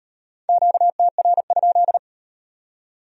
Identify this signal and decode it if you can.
Morse code, heard as a keyed tone, about 38 wpm, 700 Hz. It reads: QTR?